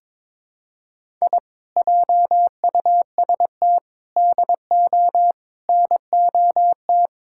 Morse code, heard as a keyed tone, about 21 wpm, 700 Hz. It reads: I JUST DO NOT
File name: practice / 22wpm / Sentences with 4-Word Phrases - 22wpm